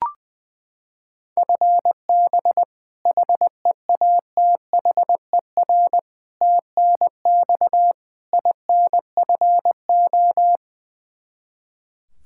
Morse code, heard as a keyed tone, about 20 wpm, 700 Hz. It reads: FB HEATHER TNX INFO
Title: Practice QSO - Exchange 3 - More Info - 20wpm